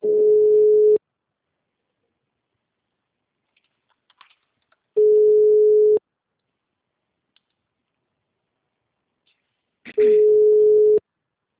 Ringback tone (Europe, 425 Hz).